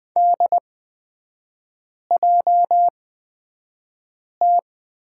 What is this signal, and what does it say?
Morse code, heard as a keyed tone, about 20 wpm, spaced as for 5 wpm, 700 Hz. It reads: DJT
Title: 243 Lesson 39e - Letter z - Three Characters 20wpm